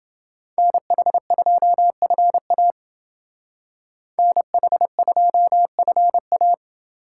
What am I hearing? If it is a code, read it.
Morse code, heard as a keyed tone, about 30 wpm, 700 Hz. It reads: NH2FA NH2FA